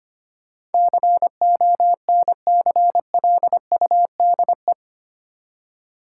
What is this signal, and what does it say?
Morse code, heard as a keyed tone, about 25 wpm, 700 Hz. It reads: CONCLUDE
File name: 8 Letter Words - First 3 Letters Given - 25wpm